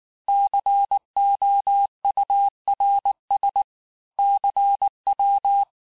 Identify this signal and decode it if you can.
Morse code, heard as a keyed tone, about 19 wpm, 800 Hz. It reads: COURS CW
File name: cours-du-mercredi / audios / cours-869-texte-CHIFFRE-vitesse-18-MPM